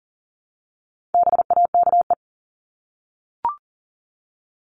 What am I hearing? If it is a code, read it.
Morse code, heard as a keyed tone, about 40 wpm, 700 Hz. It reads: BAKE
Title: Top 1751-2000 Words 40wpm